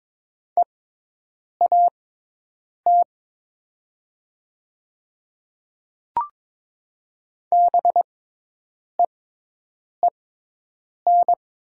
Morse code, heard as a keyed tone, about 22 wpm, spaced as for 8 wpm, 700 Hz. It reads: EAT BEEN
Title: 6x Character Spacing - Top 500 Words - 22wpm